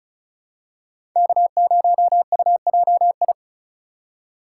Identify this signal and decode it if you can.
Morse code, heard as a keyed tone, about 36 wpm, 700 Hz. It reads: K0UJI
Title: USA Call Signs - 5 Characters - 3x Character Spacing - 35wpm